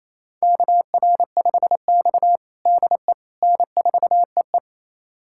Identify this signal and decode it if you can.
Morse code, heard as a keyed tone, about 28 wpm, 700 Hz. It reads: KR5X DE N4EE
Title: USA Callsign DE Callsign - 4 Characters - 28wpm